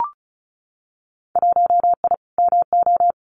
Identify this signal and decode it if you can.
Morse code, heard as a keyed tone, about 36 wpm, 700 Hz. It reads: 1I MO
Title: Winter Field Day Exchanges Encore 35wpm